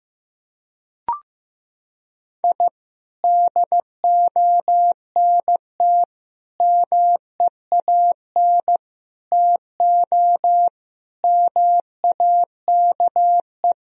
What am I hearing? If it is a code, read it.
Morse code, heard as a keyed tone, about 15 wpm, 700 Hz. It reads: I DONT MEAN TO MAKE
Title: Sentences from Top 1000 Words - Review Part 1 - 15wpm